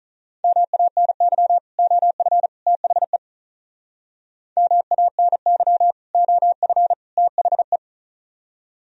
Morse code, heard as a keyed tone, about 44 wpm, 700 Hz. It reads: MANY OF THE MANY OF THE